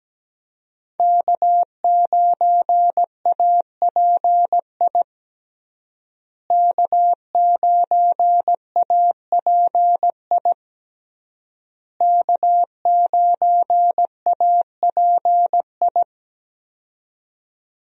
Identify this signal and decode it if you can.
Morse code, heard as a keyed tone, about 17 wpm, 700 Hz. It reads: K9API K9API K9API
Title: USA Call Signs - 5 Characters - Repeated 3x - 17wpm